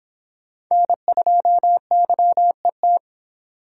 Morse code, heard as a keyed tone, about 25 wpm, 700 Hz. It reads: N2YET